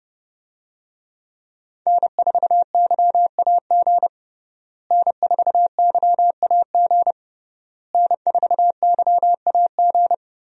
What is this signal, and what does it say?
Morse code, heard as a keyed tone, about 32 wpm, 700 Hz. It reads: N4YAG N4YAG N4YAG
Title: USA Call Signs - 5 Characters - Repeated 3x - 30wpm